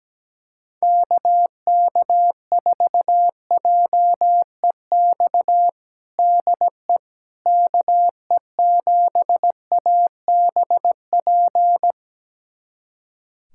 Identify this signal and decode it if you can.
Morse code, heard as a keyed tone, about 17 wpm, 700 Hz. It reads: KK4JEX DE KE7ABP
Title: USA Callsign DE Callsign - 6 Characters - 17wpm